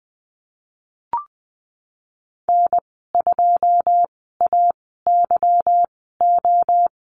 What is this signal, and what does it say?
Morse code, heard as a keyed tone, about 20 wpm, spaced as for 15 wpm, 700 Hz. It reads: N2AYO
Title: 213 Lesson 34g - Number 8 - Call Signs 20wpm